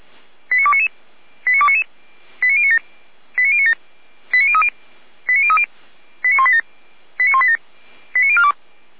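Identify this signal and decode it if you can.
Five-tone selective call (ZVEI), heard as quick runs of five tones, tones of about 80 ms. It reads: 89299 89299 89097 89097 89030 89030 89187 89187 89042
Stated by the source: ZVEI five-tone selective call received by SDR.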